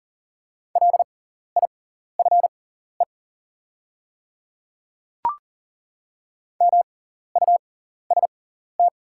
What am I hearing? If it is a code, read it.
Morse code, heard as a keyed tone, about 40 wpm, spaced as for 14 wpm, 700 Hz. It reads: LIFE MUST
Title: 6x Character Spacing - Top 300 Words - 40wpm